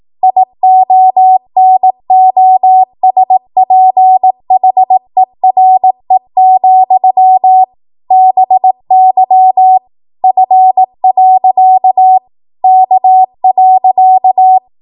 Morse code, heard as a keyed tone, about 18 wpm, 750 Hz. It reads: IONOSPHERE, BY F. K.
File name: Morse / 18WPM